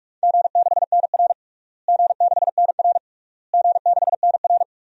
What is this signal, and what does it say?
Morse code, heard as a keyed tone, about 41 wpm, 700 Hz. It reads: G6NR G6NR G6NR
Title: DXCC-100 Call Signs - 4 Characters - Repeated 3x - 45wpm